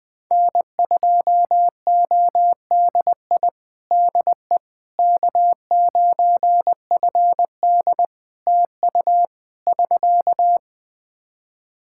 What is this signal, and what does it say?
Morse code, heard as a keyed tone, about 20 wpm, 700 Hz. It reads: N2ODI DE K9FD TU <SK>
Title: Practice QSO - Exchange 4 - Ending - 20wpm